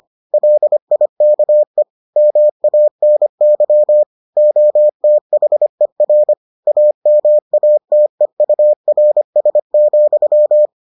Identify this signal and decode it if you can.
Morse code, heard as a keyed tone, about 25 wpm, 600 Hz. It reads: LIKE MANY OTHER AMATEURS,